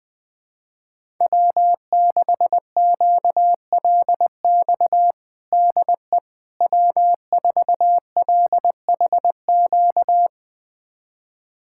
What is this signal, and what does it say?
Morse code, heard as a keyed tone, about 20 wpm, 700 Hz. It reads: W6QLX DE W4LHQ
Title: USA Callsign DE Callsign - 5 Characters - 20wpm